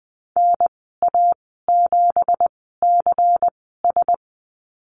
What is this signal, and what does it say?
Morse code, heard as a keyed tone, about 20 wpm, spaced as for 14 wpm, 700 Hz. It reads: NA7CS
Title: 179 Lesson 29g - Slash - Call Signs 20wpm